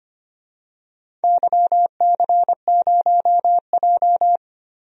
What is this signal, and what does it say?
Morse code, heard as a keyed tone, about 25 wpm, 700 Hz. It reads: YC0J
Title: DXCC-100 Call Signs - 4 Characters - 25wpm